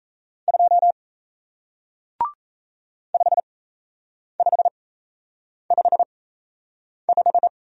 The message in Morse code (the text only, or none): 2 5555